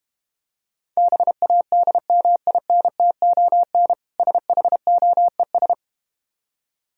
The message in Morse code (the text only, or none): BADMINTON SHOES